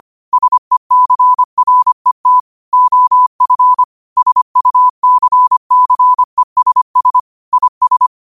Morse code, heard as a keyed tone, about 25 wpm, 1000 Hz. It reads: SECRET OF SUCCESS IS
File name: qotdfiles / 25wpm